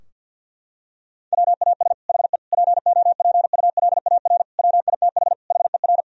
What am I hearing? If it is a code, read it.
Morse code, heard as a keyed tone, about 50 wpm, 700 Hz. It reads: WAS HE POPULAR WITH HER
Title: Sentences from Top 1000 Words - Review Part 3 - 50wpm